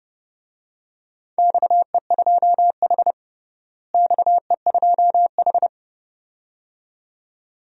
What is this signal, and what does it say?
Morse code, heard as a keyed tone, about 30 wpm, 700 Hz. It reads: XE2H XE2H
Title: DXCC-100 Call Signs - 4 Characters - Repeated 2x - 30wpm